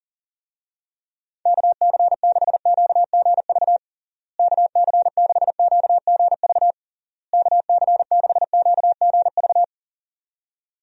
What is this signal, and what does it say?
Morse code, heard as a keyed tone, about 40 wpm, 700 Hz. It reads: KC6QGV KC6QGV KC6QGV
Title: USA Call Signs - 6 Characters - Repeated 3x - 40wpm